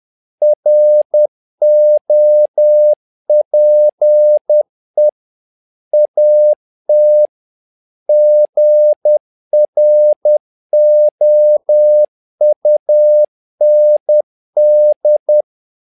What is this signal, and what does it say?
Morse code, heard as a keyed tone, about 10 wpm, 600 Hz. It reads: ROPE AT GROUND
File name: learning center / tower trouble / tower trouble 10wpm